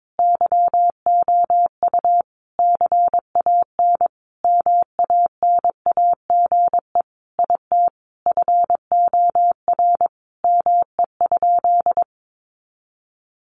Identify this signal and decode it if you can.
Morse code, heard as a keyed tone, about 22 wpm, 700 Hz. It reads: YOU CAN MANAGE IT FOR ME?
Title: Sentences from Top 1400 Words 22wpm